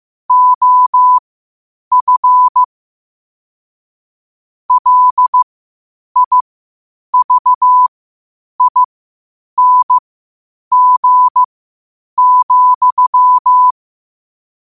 Morse code, heard as a keyed tone, about 15 wpm, spaced as for 8 wpm, 1000 Hz. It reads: OF LIVING,
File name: qotdfiles / 8wpm